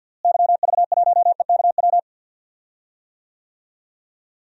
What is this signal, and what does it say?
Morse code, heard as a keyed tone, about 52 wpm, 700 Hz. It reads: YV1EKW
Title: DXCC-100 Call Signs - 6 Characters - Repeated 3x - 50wpm